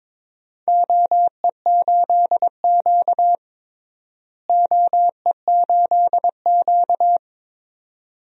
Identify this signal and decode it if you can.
Morse code, heard as a keyed tone, about 22 wpm, 700 Hz. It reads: OE8Q OE8Q
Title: DXCC-100 Call Signs - 4 Characters - Repeated 2x - 22wpm